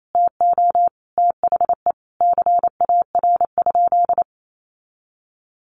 Morse code, heard as a keyed tone, about 28 wpm, 700 Hz. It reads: TO THE CAR?
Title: Sentences from Top 1200 Words 28wpm